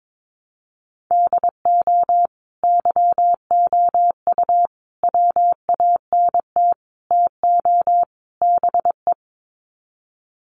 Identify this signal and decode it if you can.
Morse code, heard as a keyed tone, about 22 wpm, 700 Hz. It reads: DO YOU WANT TO BE
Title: Sets of 5 Words Encore2 22wpm